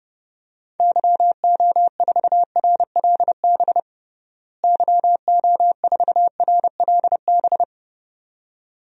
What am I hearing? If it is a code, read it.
Morse code, heard as a keyed tone, about 30 wpm, 700 Hz. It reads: YO4RLB YO4RLB
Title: DXCC-100 Call Signs - 6 Characters - Repeated 2x - 30wpm